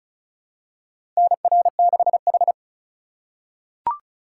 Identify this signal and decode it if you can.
Morse code, heard as a keyed tone, about 36 wpm, 700 Hz. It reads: NR6H
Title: USA Call Signs - 4 Characters - 4x Character Spacing - 35wpm